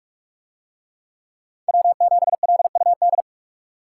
Morse code, heard as a keyed tone, about 44 wpm, 700 Hz. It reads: W7LUD